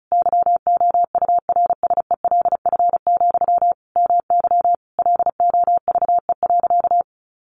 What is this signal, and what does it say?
Morse code, heard as a keyed tone, about 34 wpm, 700 Hz. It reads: YOURSELF, MY LOVE.
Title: Sentences from Top 1100 Words 35wpm